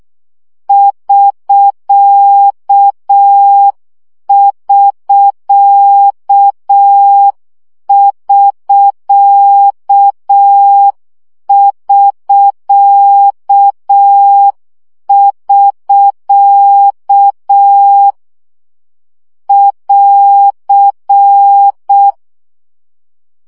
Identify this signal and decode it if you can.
Morse code, heard as a keyed tone, about 6 wpm, 800 Hz. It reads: <SK><SK><SK><SK><SK> <AR>
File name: CW gyakorlas sk 5 baud